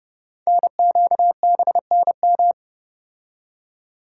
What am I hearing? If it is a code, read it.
Morse code, heard as a keyed tone, about 29 wpm, 700 Hz. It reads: NQBNM
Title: Code Groups - 5 Letters - 30wpm